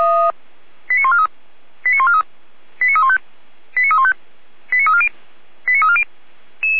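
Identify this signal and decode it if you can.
Five-tone selective call (ZVEI), heard as quick runs of five tones, tones of about 70 ms. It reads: 89143 89143 89315 89315 89340 89340
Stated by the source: ZVEI five-tone selective call received on SDR.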